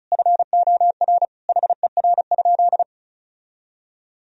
Morse code, heard as a keyed tone, about 36 wpm, 700 Hz. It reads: FOR HER?
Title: Sentences from Top 1000 Words - Review Part 1 - 35wpm